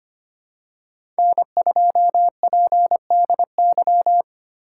Morse code, heard as a keyed tone, about 25 wpm, 700 Hz. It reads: N2PDY
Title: USA Call Signs - 5 Characters - 4x Character Spacing - 25wpm